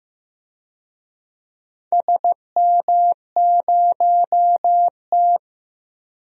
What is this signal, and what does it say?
Morse code, heard as a keyed tone, about 15 wpm, 700 Hz. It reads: SM0T
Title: DXCC-100 Call Signs - 4 Characters - 15wpm